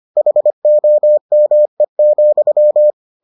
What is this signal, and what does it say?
Morse code, heard as a keyed tone, about 25 wpm, 600 Hz. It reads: HOME,